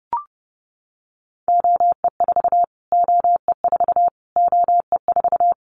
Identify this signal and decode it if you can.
Morse code, heard as a keyed tone, about 30 wpm, 700 Hz. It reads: OE4 OE4 OE4
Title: Callsign Prefixes for Top 25 DXCC - Repeated 3x - 30wpm